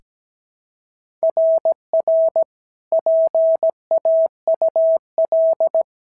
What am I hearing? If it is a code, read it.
Morse code, heard as a keyed tone, about 16 wpm, 650 Hz. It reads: RR PAUL
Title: SOTA QSO Practice 17wpm